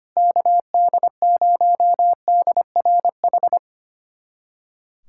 Morse code, heard as a keyed tone, about 25 wpm, 700 Hz. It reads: KD0DRH
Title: USA Callsign DE Callsign - 6 Characters - 25wpm